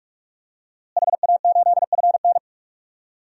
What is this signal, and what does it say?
Morse code, heard as a keyed tone, about 46 wpm, 700 Hz. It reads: HA8FN
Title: DXCC-100 Call Signs - 5 Characters - Repeated 3x - 45wpm